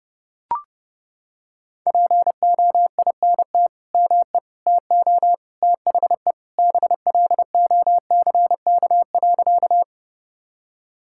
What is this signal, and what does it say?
Morse code, heard as a keyed tone, about 32 wpm, 700 Hz. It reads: POINT ME TO THE BLOCK.
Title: Sentences from Top 1600 Words 30wpm